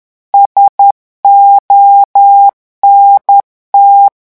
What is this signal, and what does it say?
Morse code, heard as a keyed tone, about 10 wpm, 800 Hz. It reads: SONT
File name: morse mp3 / revision9 citations humoristiques vitesse 10 mots minute